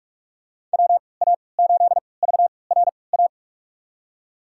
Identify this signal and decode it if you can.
Morse code, heard as a keyed tone, about 46 wpm, spaced as for 26 wpm, 700 Hz. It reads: WA8VRA
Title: USA Call Signs - 6 Characters - 3x Character Spacing - 45wpm